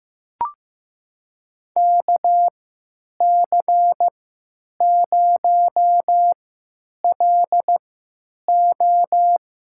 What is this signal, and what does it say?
Morse code, heard as a keyed tone, about 15 wpm, spaced as for 9 wpm, 700 Hz. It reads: KC0LO
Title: USA Call Signs - 3x Character Spacing - 15wpm